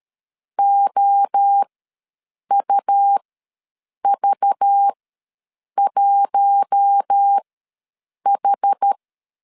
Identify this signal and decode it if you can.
Morse code, heard as a keyed tone, about 13 wpm, spaced as for 7 wpm, 800 Hz. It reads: OUV1H
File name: LEZIONI / 7 GF9Z / 7 G ws 12wpm 3140